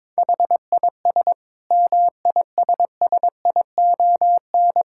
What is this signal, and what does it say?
Morse code, heard as a keyed tone, about 22 wpm, 700 Hz. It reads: HIS MISSION
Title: Sentences from Top 1600 Words 22wpm